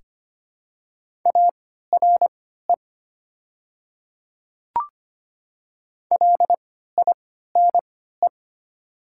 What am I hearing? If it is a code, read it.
Morse code, heard as a keyed tone, about 25 wpm, spaced as for 14 wpm, 700 Hz. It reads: ARE LINE
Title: 3x Character Spacing - Top 300 Words - 25wpm